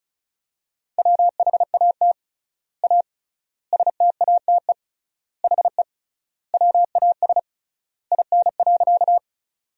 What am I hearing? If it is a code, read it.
Morse code, heard as a keyed tone, about 36 wpm, 700 Hz. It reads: WHAT A STATE HE WAS IN.
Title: 3x Word-Spacing - Sentences from Top 400 Words - 35wpm